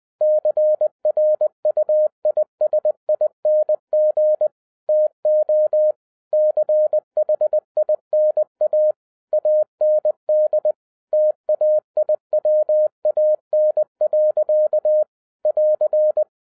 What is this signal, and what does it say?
Morse code, heard as a keyed tone, about 20 wpm, 610 Hz. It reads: CRUISING TO CHINA AND TAIWAN. <AR>
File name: 2011-07-mp3 / n18 GREETINGS FROM SINGAPORE 20wpm